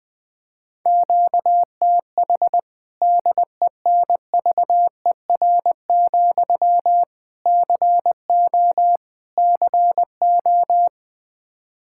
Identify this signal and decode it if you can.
Morse code, heard as a keyed tone, about 20 wpm, 700 Hz. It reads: QTH DENVER, CO CO
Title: Practice QSO - Exchange 1 - Beginning - 20wpm